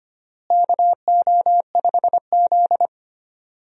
Morse code, heard as a keyed tone, about 25 wpm, 700 Hz. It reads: KO5Z
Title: USA Call Signs - 4 Characters - Repeated 3x - 25wpm